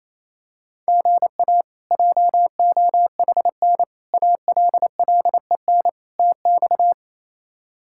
Morse code, heard as a keyed tone, about 28 wpm, 700 Hz. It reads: GA JOHN ALLEN TX
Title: K1USN SST QSO Practice - 28wpm